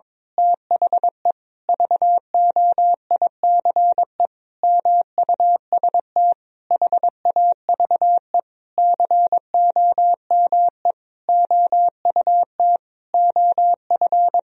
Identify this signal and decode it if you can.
Morse code, heard as a keyed tone, about 21 wpm, 700 Hz. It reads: THE VOICE MUST HAVE COME OUT OF
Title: Sentences from Top 1000 Words - Review Part 2 - 22wpm